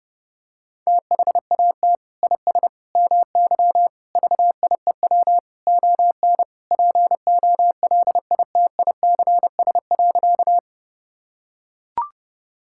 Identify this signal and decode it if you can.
Morse code, heard as a keyed tone, about 32 wpm, 700 Hz. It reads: THAT IS MY VIEW ON POLITICS.
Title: Sentences from Top 1000 Words - Review Part 6 - 30wpm